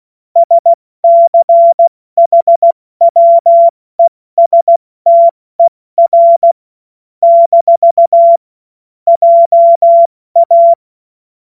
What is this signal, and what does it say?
Morse code, heard as a keyed tone, about 16 wpm, 680 Hz. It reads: SCHWESTER - JA
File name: Morsekurs / Witze / Witz 07. September 2021, 70 F80